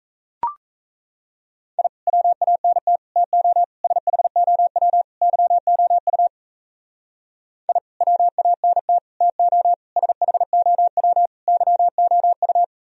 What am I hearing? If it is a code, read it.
Morse code, heard as a keyed tone, about 34 wpm, 700 Hz. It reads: I WANT TO SHOW YOU I WANT TO SHOW YOU
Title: Speed-Racing - Sets of 5 Words 28wpm